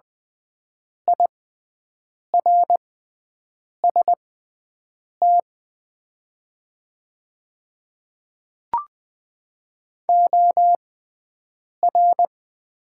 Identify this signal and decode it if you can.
Morse code, heard as a keyed tone, about 20 wpm, spaced as for 7 wpm, 700 Hz. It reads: IRST OR
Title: 6x Character Spacing - Top 100 Words - 20wpm